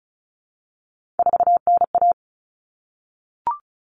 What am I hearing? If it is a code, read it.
Morse code, heard as a keyed tone, about 34 wpm, 700 Hz. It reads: 4NA
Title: Common Cut Numbers - 3-Digit 35wpm